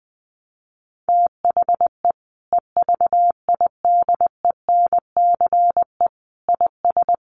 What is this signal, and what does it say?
Morse code, heard as a keyed tone, about 20 wpm, 700 Hz. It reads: THE EVIDENCE IS
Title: Sentences from Top 1300 Words 20wpm